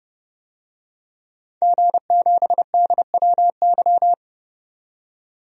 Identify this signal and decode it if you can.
Morse code, heard as a keyed tone, about 32 wpm, 700 Hz. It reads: G7DWY
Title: DXCC-100 Call Signs - 5 Characters - 30wpm